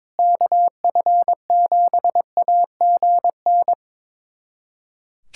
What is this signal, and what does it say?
Morse code, heard as a keyed tone, about 22 wpm, 700 Hz. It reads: KF7AGN
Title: USA Callsign DE Callsign - 6 Characters - 22wpm